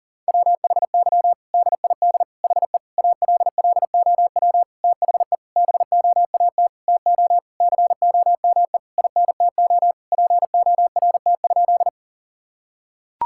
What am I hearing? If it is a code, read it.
Morse code, heard as a keyed tone, about 40 wpm, 700 Hz. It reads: WHY DID HE ALLOW THE BOAT TO COME INTO PORT?
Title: Sentences from Top 1000 Words - Review Part 5 - 40wpm